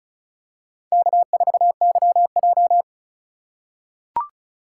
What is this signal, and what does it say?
Morse code, heard as a keyed tone, about 36 wpm, 700 Hz. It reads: K4YJ
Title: USA Call Signs - 4 Characters - 4x Character Spacing - 35wpm